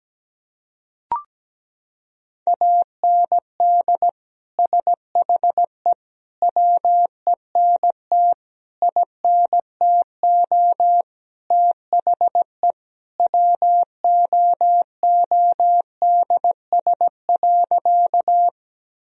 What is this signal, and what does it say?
Morse code, heard as a keyed tone, about 17 wpm, 700 Hz. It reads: AND SHE WENT INTO THE WOODS.
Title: Sentences from Top 1000 Words - Review Part 2 - 17wpm